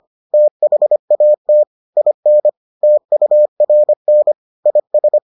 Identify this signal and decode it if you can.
Morse code, heard as a keyed tone, about 25 wpm, 600 Hz. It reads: THAT IN TURN IS